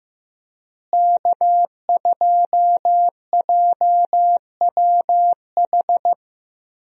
Morse code, heard as a keyed tone, about 15 wpm, 700 Hz. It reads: K2JWH
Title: USA Call Signs - 5 Characters - Repeated 2x - 15wpm